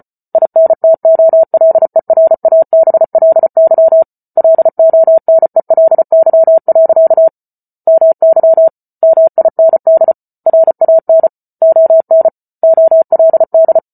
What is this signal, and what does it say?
Morse code, heard as a keyed tone, about 34 wpm, 650 Hz. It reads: INTOLERABLY LONELY. MY MIND RAN ON OLD